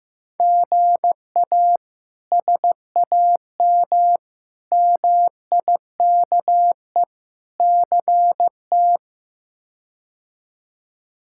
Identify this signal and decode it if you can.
Morse code, heard as a keyed tone, about 16 wpm, 700 Hz. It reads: GA SAM MIKE CT
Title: K1USN SST QSO Practice - 15wpm